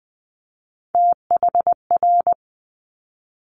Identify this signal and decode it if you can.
Morse code, heard as a keyed tone, about 20 wpm, 700 Hz. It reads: THR
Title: Multi-letter Phonemes 20wpm